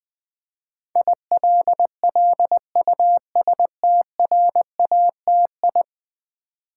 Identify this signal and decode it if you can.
Morse code, heard as a keyed tone, about 20 wpm, 700 Hz. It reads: ILLUSTRATI
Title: Build A Word - 5 Letters At A Time - 12-Letter Words - 20wpm